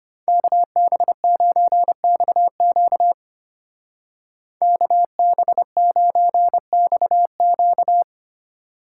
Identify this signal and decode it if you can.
Morse code, heard as a keyed tone, about 30 wpm, 700 Hz. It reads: KB9XQ KB9XQ
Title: Speed Racing - USA 2x2 Call Signs 25wpm